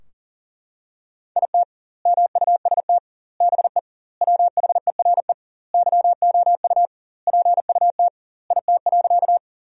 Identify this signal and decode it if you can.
Morse code, heard as a keyed tone, about 40 wpm, 700 Hz. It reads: IT MUST BE WHERE YOU PUT IT.
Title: 2x Word-Spacing - Sentences from Top 200 Words - 40wpm